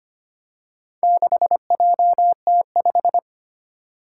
Morse code, heard as a keyed tone, about 25 wpm, 700 Hz. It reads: 6JT5
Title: Four Challenging Letter-Number Combinations 25wpm